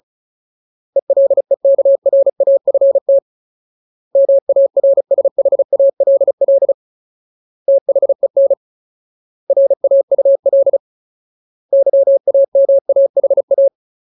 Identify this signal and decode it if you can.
Morse code, heard as a keyed tone, about 35 wpm, 550 Hz. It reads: ELEKRAFT MARSHALL THEN RAUL YAMAHA
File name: Ultimate Word List / 35WPM Ultimate